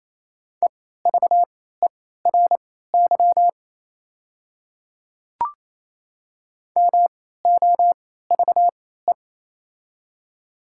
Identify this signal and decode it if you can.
Morse code, heard as a keyed tone, about 28 wpm, spaced as for 16 wpm, 700 Hz. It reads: EVERY MOVE
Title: 3x Character Spacing - Top 500 Words - 28wpm